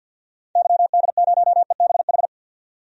Morse code, heard as a keyed tone, about 52 wpm, 700 Hz. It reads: YD0EBH
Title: DXCC-100 Call Signs - 6 Characters - 50wpm